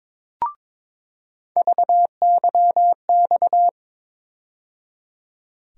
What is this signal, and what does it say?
Morse code, heard as a keyed tone, about 22 wpm, 700 Hz. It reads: VYX
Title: USA Call Sign Postfixes - 3 Letters 22wpm